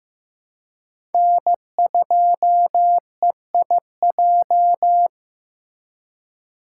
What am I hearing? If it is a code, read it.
Morse code, heard as a keyed tone, about 15 wpm, 700 Hz. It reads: N2EIJ
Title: USA Call Signs - 5 Characters - 2x Character Spacing - 15wpm